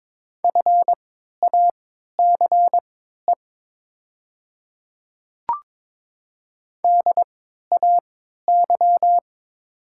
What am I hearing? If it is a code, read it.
Morse code, heard as a keyed tone, about 22 wpm, spaced as for 13 wpm, 700 Hz. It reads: FACE DAY
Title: 3x Character Spacing - Top 500 Words - 22wpm